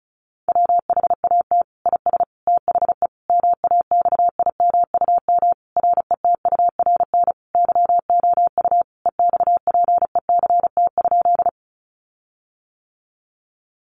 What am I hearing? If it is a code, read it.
Morse code, heard as a keyed tone, about 34 wpm, 700 Hz. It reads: WHAT IS THE MAXIMUM RETURN YOU EXPECT?
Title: Sentences from Top 1300 Words 35wpm